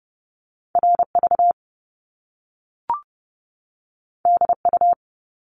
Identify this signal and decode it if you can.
Morse code, heard as a keyed tone, about 30 wpm, 700 Hz. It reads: RV DU
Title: Top Two Letter Combinations in Words 30wpm